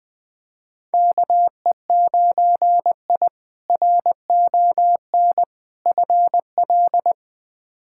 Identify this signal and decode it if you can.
Morse code, heard as a keyed tone, about 21 wpm, 700 Hz. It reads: KE9I RON FL
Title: K1USN SST QSO Practice - 20wpm